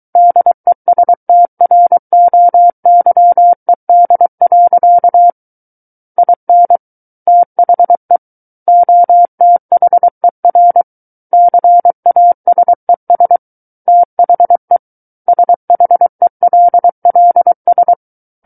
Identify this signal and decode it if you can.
Morse code, heard as a keyed tone, about 23 wpm, 700 Hz. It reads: DESTROYED. IN THE OTHER CASES THE SHELLS
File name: Book One / 14-Chapter 14 - 23 WPM